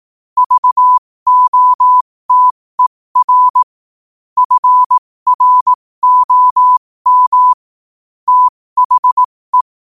Morse code, heard as a keyed tone, about 18 wpm, spaced as for 15 wpm, 1000 Hz. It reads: VOTER FROM THE
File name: qotdfiles / 15wpm